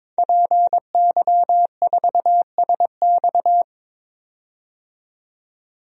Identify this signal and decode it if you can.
Morse code, heard as a keyed tone, about 21 wpm, 700 Hz. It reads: PY4SX
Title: DXCC-100 Call Signs - 5 Characters - 22wpm